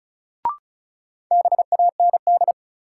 Morse code, heard as a keyed tone, about 36 wpm, 700 Hz. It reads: BAND
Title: Sentences from Top 1600 Words - New Word Drill - 35wpm